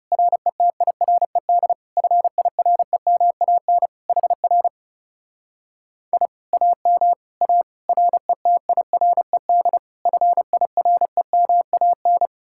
Morse code, heard as a keyed tone, about 39 wpm, 700 Hz. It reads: RETIRED FIREMAN HR I AM A RETIRED FIREMAN